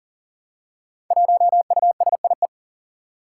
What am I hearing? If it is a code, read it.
Morse code, heard as a keyed tone, about 40 wpm, 700 Hz. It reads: 1USIE
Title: Five Challenging Letter-Number Combinations 40wpm